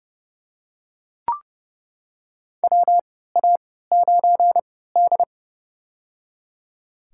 Morse code, heard as a keyed tone, about 30 wpm, spaced as for 17 wpm, 700 Hz. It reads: WA9D
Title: USA Call Signs - 4 Characters - 3x Character Spacing - 30wpm